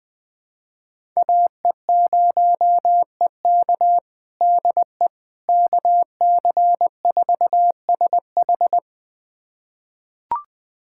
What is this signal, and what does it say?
Morse code, heard as a keyed tone, about 20 wpm, 700 Hz. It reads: AE0EK DE KC4SH
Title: USA Callsign DE Callsign - 5 Characters - 20wpm